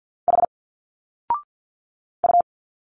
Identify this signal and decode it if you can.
Morse code, heard as a keyed tone, about 49 wpm, 700 Hz. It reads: H U
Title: Single Letters 50wpm